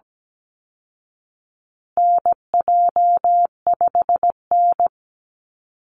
Morse code, heard as a keyed tone, about 17 wpm, 700 Hz. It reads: NJ5N
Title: Call Signs Repeated 3x - 17wpm